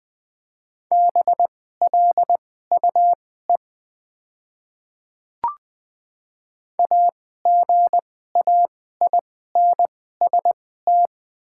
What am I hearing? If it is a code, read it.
Morse code, heard as a keyed tone, about 20 wpm, spaced as for 14 wpm, 700 Hz. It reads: BLUE AGAINST
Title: 2x Character Spacing - Top 500 Words - 20wpm